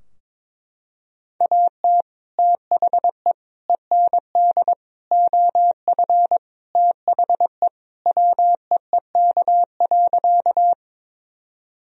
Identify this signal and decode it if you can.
Morse code, heard as a keyed tone, about 21 wpm, 700 Hz. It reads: AT THE END OF THE WEEK.
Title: Sentences from Top 1000 Words - Review Part 2 - 22wpm